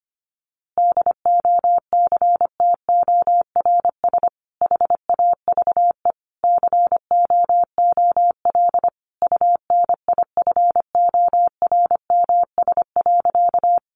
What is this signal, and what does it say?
Morse code, heard as a keyed tone, about 25 wpm, 700 Hz. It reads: DOCTORS HAVE COOL UNIFORMS.
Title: Easy Phrases 25wpm